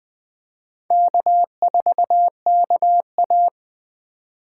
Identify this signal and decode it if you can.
Morse code, heard as a keyed tone, about 20 wpm, 700 Hz. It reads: K4KA
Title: USA Call Signs - 4 Characters - 3x Character Spacing - 20wpm